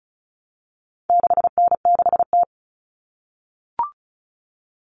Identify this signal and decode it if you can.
Morse code, heard as a keyed tone, about 34 wpm, 700 Hz. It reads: 6N6T
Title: All Cut Numbers - 4-Digit 35wpm